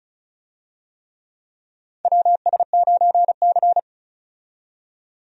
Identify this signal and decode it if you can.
Morse code, heard as a keyed tone, about 34 wpm, 700 Hz. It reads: WS9C
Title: USA Call Signs - 4 Characters - 35wpm